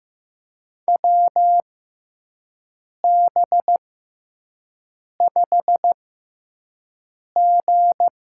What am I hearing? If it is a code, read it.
Morse code, heard as a keyed tone, about 15 wpm, spaced as for 5 wpm, 700 Hz. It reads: WB5G
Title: USA Call Signs - 6 Characters - 6x Character Spacing - 15wpm